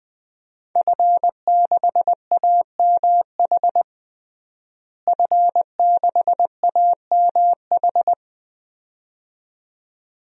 Morse code, heard as a keyed tone, about 19 wpm, 700 Hz. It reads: F6AMH F6AMH
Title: DXCC-100 Call Signs - 5 Characters - Repeated 2x - 20wpm